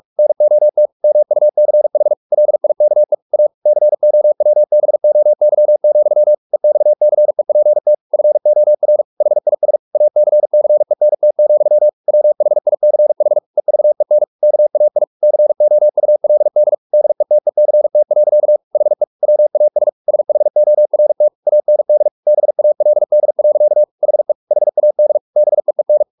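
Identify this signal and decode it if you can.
Morse code, heard as a keyed tone, about 44 wpm, 600 Hz. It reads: NOT MUCH LIKE A COWBOY, EXCEPT FOR HIS ACCENT, WHICH EVEN KAI COULD DETECT. HE WAS SHORT AND BALD. HE HAD BEEN
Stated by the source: Morse practice at 45 wpm, the novel Never by Ken Follett in English, from CW med Gnister.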